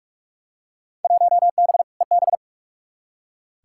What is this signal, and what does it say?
Morse code, heard as a keyed tone, about 47 wpm, 700 Hz. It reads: 1B EB
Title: Field Day Exchanges Encore - 45wpm